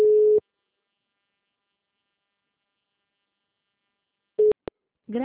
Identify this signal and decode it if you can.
Ringback tone (Europe, 425 Hz).